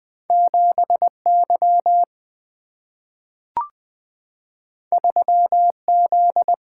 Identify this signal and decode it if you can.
Morse code, heard as a keyed tone, about 20 wpm, 700 Hz. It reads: 7Y 3Z
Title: Two Letter-Number Combinations - 20wpm